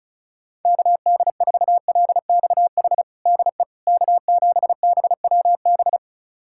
Morse code, heard as a keyed tone, about 36 wpm, 700 Hz. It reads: KD4LXH DE K7BWB
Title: USA Callsign DE Callsign 35wpm